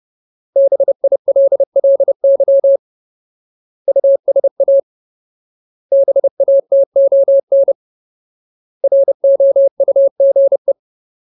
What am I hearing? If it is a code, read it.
Morse code, heard as a keyed tone, about 30 wpm, 550 Hz. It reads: BILLY USA BATON ROUGE
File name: Ultimate Word List / 30WPM Ultimate